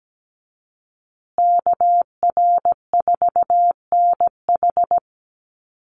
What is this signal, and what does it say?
Morse code, heard as a keyed tone, about 17 wpm, 700 Hz. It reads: KR4NH
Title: USA Call Signs Repeated 3x 17wpm